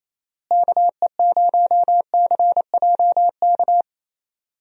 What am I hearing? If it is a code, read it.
Morse code, heard as a keyed tone, about 28 wpm, 700 Hz. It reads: KE0CJK
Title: USA Call Signs - 6 Characters - Repeated 2x - 28wpm